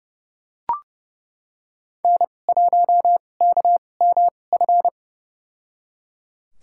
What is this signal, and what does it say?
Morse code, heard as a keyed tone, about 30 wpm, spaced as for 22 wpm, 700 Hz. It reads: N1KMF
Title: USA Call Signs - 2x Character Spacing - 30wpm